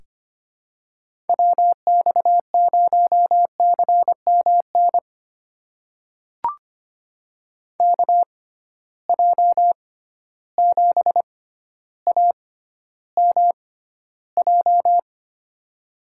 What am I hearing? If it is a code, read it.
Morse code, heard as a keyed tone, about 25 wpm, spaced as for 13 wpm, 700 Hz. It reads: WX0CMN KJ7AMJ
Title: USA Call Signs - 6 Characters - 6x Character Spacing - 25wpm